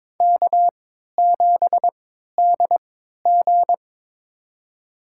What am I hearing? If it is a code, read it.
Morse code, heard as a keyed tone, about 23 wpm, spaced as for 13 wpm, 700 Hz. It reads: K7DG
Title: USA Call Signs - 4 Characters - 3x Character Spacing - 22wpm